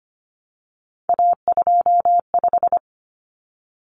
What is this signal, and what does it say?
Morse code, heard as a keyed tone, about 25 wpm, 700 Hz. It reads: A25
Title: Common Cut Numbers - 3-Digit 25wpm